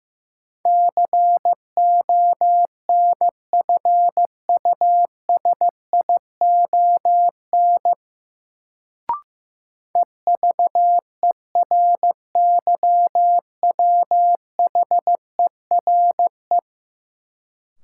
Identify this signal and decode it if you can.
Morse code, heard as a keyed tone, about 15 wpm, 700 Hz. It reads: CONFUSION EVERYWHERE
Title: Sentences from Top 1600 Words - New Word Drill - 15wpm